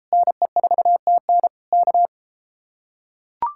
Morse code, heard as a keyed tone, about 33 wpm, 700 Hz. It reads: NE4TN K